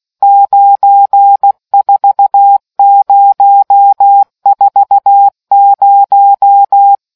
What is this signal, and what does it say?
Morse code, heard as a keyed tone, about 16 wpm, 800 Hz. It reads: 94040